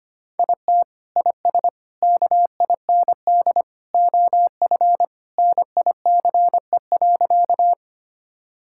Morse code, heard as a keyed tone, about 25 wpm, 700 Hz. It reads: IT IS KIND OF NICE.
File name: practice / 25wpm / Sentences with 2-Word Phrases - 25wpm